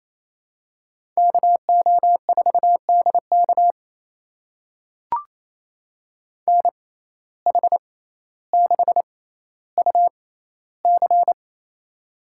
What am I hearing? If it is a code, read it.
Morse code, heard as a keyed tone, about 28 wpm, spaced as for 14 wpm, 700 Hz. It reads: KO4DK NH6UC
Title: USA Call Signs - 6x Character Spacing - 28wpm